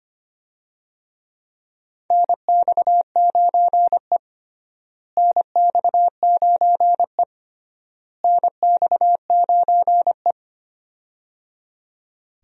Morse code, heard as a keyed tone, about 25 wpm, 700 Hz. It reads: NX9E NX9E NX9E
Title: USA Call Signs - 4 Characters - Repeated 3x - 25wpm